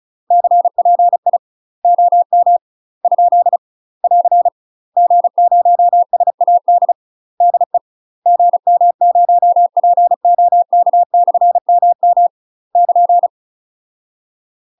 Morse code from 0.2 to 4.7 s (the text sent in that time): CPI OM ? <AR>